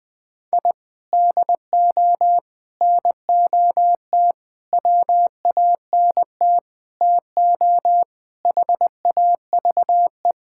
Morse code, heard as a keyed tone, about 21 wpm, 700 Hz. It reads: I DO NOT WANT TO HAVE
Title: Sentences with 5-Word Phrases - 20wpm